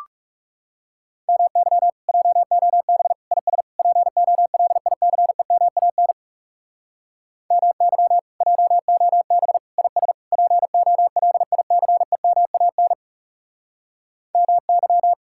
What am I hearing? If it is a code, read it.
Morse code, heard as a keyed tone, about 36 wpm, 700 Hz. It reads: MY JOB IS POLICEMAN MY JOB IS POLICEMAN MY
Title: Speed-Racing Common QSO Phrases - 30wpm